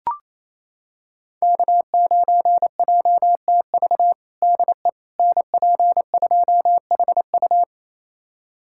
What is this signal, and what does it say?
Morse code, heard as a keyed tone, about 28 wpm, 700 Hz. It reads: K9JTV DE NP2HU
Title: USA Callsign DE Callsign - 5 Characters - 28wpm